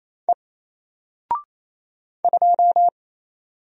Morse code, heard as a keyed tone, about 27 wpm, 700 Hz. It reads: E 2
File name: Single-Letter-Number-28wpm